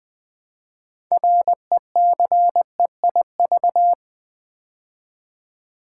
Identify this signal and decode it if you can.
Morse code, heard as a keyed tone, about 20 wpm, 700 Hz. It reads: RECEIV
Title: Build A Word - 3 Letters At A Time - 8-Letter Words - 20wpm